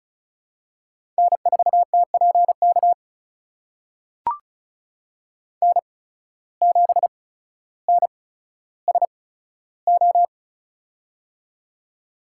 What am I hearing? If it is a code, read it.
Morse code, heard as a keyed tone, about 34 wpm, spaced as for 15 wpm, 700 Hz. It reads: N4TPK N7NSO
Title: USA Call Signs - 5 Characters - 8x Character Spacing - 35wpm